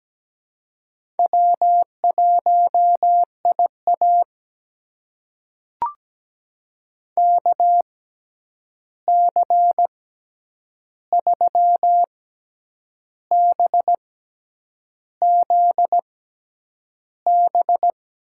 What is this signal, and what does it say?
Morse code, heard as a keyed tone, about 17 wpm, spaced as for 8 wpm, 700 Hz. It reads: W1IA KC3BZB
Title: USA Call Signs - 6x Character Spacing - 17wpm